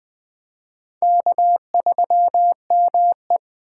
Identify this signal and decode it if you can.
Morse code, heard as a keyed tone, about 21 wpm, 700 Hz. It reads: K3ME
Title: Speed Racing - USA Call Signs 17wpm